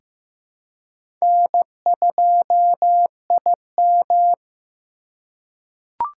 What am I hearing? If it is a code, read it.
Morse code, heard as a keyed tone, about 15 wpm, 700 Hz. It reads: N2IM
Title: USA Call Signs - 4x Character Spacing - 15wpm